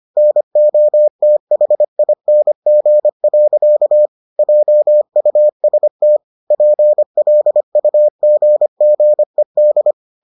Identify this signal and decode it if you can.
Morse code, heard as a keyed tone, about 25 wpm, 600 Hz. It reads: NOTHING. JUST PLUGGED